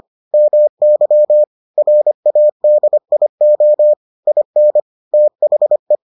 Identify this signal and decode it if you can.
Morse code, heard as a keyed tone, about 25 wpm, 600 Hz. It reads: MY RADIO IN THE